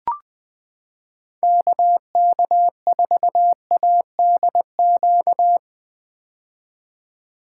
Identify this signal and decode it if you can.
Morse code, heard as a keyed tone, about 21 wpm, 700 Hz. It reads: KK4ADQ
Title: 2023 POTA Hunter Call Signs - 20wpm